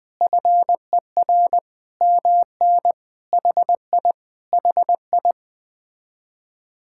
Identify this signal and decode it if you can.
Morse code, heard as a keyed tone, about 20 wpm, 700 Hz. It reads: FER MN HI HI
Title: Speed-Racing Common QSO Phrases - 17wpm